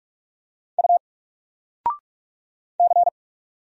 Morse code, heard as a keyed tone, about 44 wpm, 700 Hz. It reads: U C
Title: Single Letter-Number Alt 45wpm